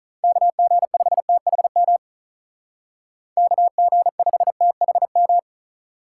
Morse code, heard as a keyed tone, about 45 wpm, 700 Hz. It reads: KG5THM KG5THM